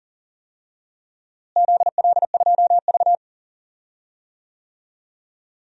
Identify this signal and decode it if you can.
Morse code, heard as a keyed tone, about 40 wpm, 700 Hz. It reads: ZL2V
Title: DXCC-100 Call Signs - 4 Characters - Repeated 2x - 40wpm